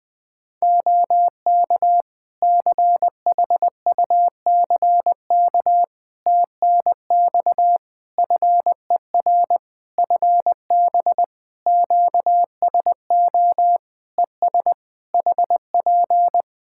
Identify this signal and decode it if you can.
Morse code, heard as a keyed tone, about 20 wpm, 700 Hz. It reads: OK CHUCK TNX FER FB QSO ES HP
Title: Practice QSO - Exchange 4 - Ending - 20wpm